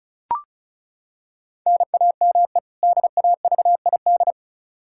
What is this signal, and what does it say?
Morse code, heard as a keyed tone, about 34 wpm, 700 Hz. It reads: NAME DAVID
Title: K1USN SST QSO Practice 35wpm